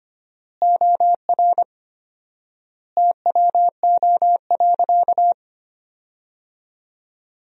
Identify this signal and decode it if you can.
Morse code, heard as a keyed tone, about 25 wpm, 700 Hz. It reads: OR TWO.
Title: 4x Word-Spacing - Sentences from Top 300 Words - 25wpm